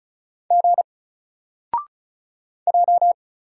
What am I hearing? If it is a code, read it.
Morse code, heard as a keyed tone, about 36 wpm, 700 Hz. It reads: G J
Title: Single Character (Letter:Number:Punctuation:Prosign) 35wpm